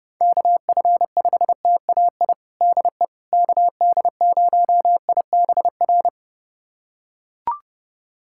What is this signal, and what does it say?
Morse code, heard as a keyed tone, about 32 wpm, 700 Hz. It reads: KF5TAI DE KD0IBR
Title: USA Callsign DE Callsign 30wpm